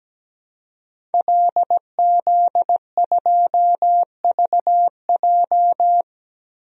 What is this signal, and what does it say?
Morse code, heard as a keyed tone, about 17 wpm, 700 Hz. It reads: LZ2VJ
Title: DXCC-100 Call Signs - 5 Characters - Repeated 2x - 17wpm